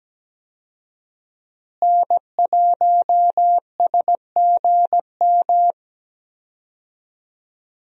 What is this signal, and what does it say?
Morse code, heard as a keyed tone, about 17 wpm, 700 Hz. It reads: N1SGM
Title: USA Call Signs - 5 Characters - Repeated 2x - 17wpm